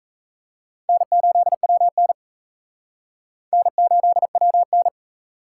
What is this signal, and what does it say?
Morse code, heard as a keyed tone, about 34 wpm, 700 Hz. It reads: N8WN N8WN
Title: Speed Racing - USA 1x2 Call Signs 28wpm